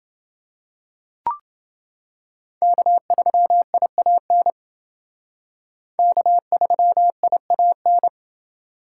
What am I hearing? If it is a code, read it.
Morse code, heard as a keyed tone, about 28 wpm, 700 Hz. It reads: K3IAN K3IAN